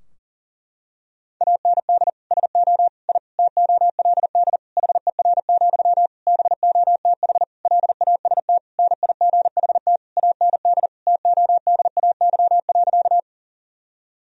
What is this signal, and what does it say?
Morse code, heard as a keyed tone, about 37 wpm, 700 Hz. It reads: AND SO I TOLD HER, BOTH LAST NIGHT AND TODAY.
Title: Sentences from Top 1000 Words - Review Part 4 - 40wpm